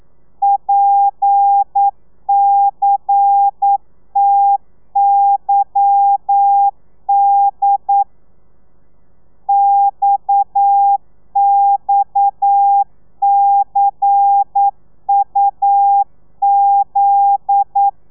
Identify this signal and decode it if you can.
Morse code, heard as a keyed tone, about 9 wpm, 800 Hz. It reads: PCTYD XXCUZ